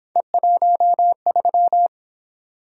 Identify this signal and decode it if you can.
Morse code, heard as a keyed tone, about 27 wpm, 700 Hz. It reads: E13
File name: Speed-Racing - Three Letter-Number Combinations - 22wpm